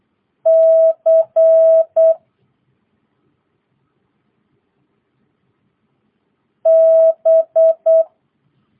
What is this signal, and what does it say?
Morse code, heard as a keyed tone, about 8 wpm, 650 Hz. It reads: C B
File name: jh8cbh / 2022cw